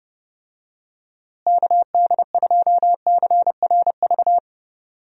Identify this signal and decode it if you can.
Morse code, heard as a keyed tone, about 31 wpm, 700 Hz. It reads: KD2CRV
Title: USA Call Signs - 6 Characters - Repeated 2x - 30wpm